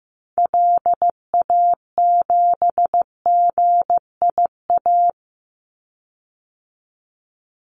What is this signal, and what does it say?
Morse code, heard as a keyed tone, about 15 wpm, 700 Hz. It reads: LA7GIA
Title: Call Signs Repeated 2x - 15wpm